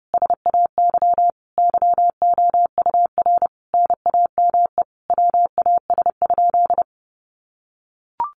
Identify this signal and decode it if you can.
Morse code, heard as a keyed tone, about 30 wpm, 700 Hz. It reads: SAY YOUR NAME WAS?
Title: Sentences from Top 200 Words 30wpm